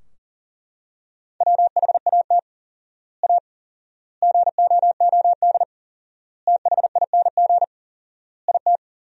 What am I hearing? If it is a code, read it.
Morse code, heard as a keyed tone, about 37 wpm, 700 Hz. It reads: WHAT A GOOD THING IT
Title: 4x Word-Spacing - Sentences from Top 200 Words - 40wpm